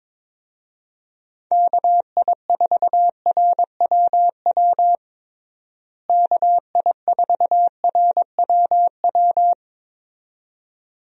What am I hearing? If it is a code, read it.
Morse code, heard as a keyed tone, about 22 wpm, 700 Hz. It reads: KI4RWW KI4RWW
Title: USA Call Signs - 6 Characters - Repeated 2x - 22wpm